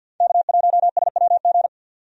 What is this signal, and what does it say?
Morse code, heard as a keyed tone, about 50 wpm, 700 Hz. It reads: K1SWG
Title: USA Call Signs - 5 Characters - 50wpm